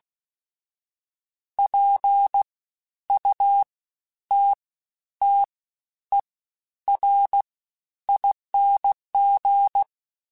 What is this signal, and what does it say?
Morse code, heard as a keyed tone, about 16 wpm, spaced as for 10 wpm, 800 Hz. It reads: PUTTERING
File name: Special INGs-mp3s / Special INGs-mp3s / SlowToFaster ing @15